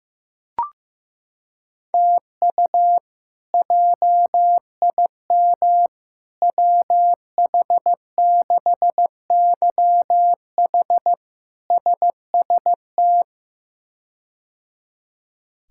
Morse code, heard as a keyed tone, about 15 wpm, 700 Hz. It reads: TU JIM WH6YH SST
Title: K1USN SST QSO Practice - 15wpm